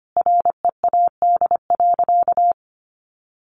Morse code, heard as a keyed tone, about 25 wpm, 700 Hz. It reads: READ.
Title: Sentences from Top 1400 Words 25wpm